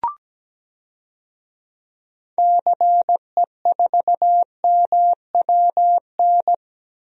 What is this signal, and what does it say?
Morse code, heard as a keyed tone, about 17 wpm, 700 Hz. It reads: CE4MWN
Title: DXCC-100 Call Signs - 6 Characters - 17wpm